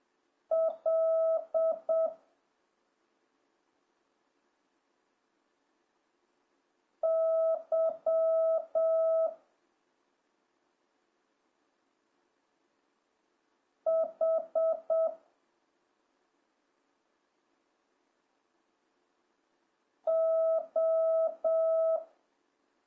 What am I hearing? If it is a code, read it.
Morse code, heard as a keyed tone, about 7 wpm, 1300 Hz. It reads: L Y H O